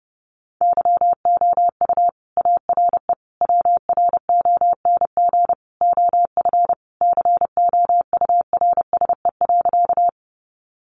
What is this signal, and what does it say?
Morse code, heard as a keyed tone, about 30 wpm, 700 Hz. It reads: YOU ARE WRONG OF COURSE.
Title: Sentences from Top 900 Words 30wpm